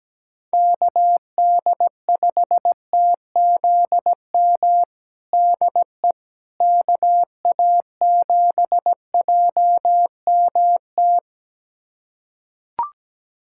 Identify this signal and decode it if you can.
Morse code, heard as a keyed tone, about 17 wpm, 700 Hz. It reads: KD5TZM DE KA7JMT
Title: USA Callsign DE Callsign 17wpm